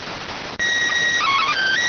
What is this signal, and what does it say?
SSTV (slow-scan television), PD 290.